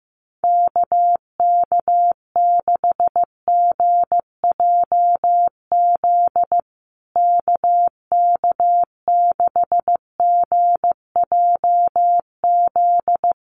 Morse code, heard as a keyed tone, about 15 wpm, 700 Hz. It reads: KK6GJZ KK6GJZ
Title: USA Call Signs Repeated 3x 15wpm